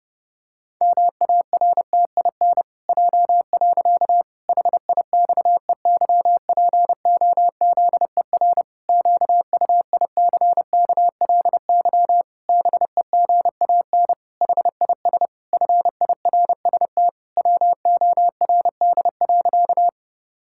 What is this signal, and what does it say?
Morse code, heard as a keyed tone, about 28 wpm, 700 Hz. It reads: MARTIN J. HIXEYPOZER QUICKLY BEGAN HIS FIRST WORD.
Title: Pangram Challenge v2 - 30wpm